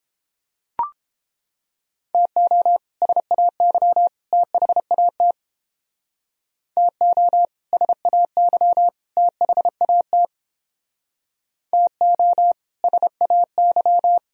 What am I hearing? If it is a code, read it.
Morse code, heard as a keyed tone, about 29 wpm, 700 Hz. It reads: TO SAY THAT TO SAY THAT TO SAY